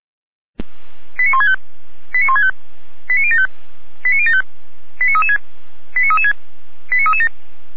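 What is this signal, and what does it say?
Five-tone selective call (ZVEI), heard as quick runs of five tones, tones of about 70 ms. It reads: 89165 89165 89075 89075 89336 89336 89337
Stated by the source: ZVEI five-tone selective call received by SDR, address 89560.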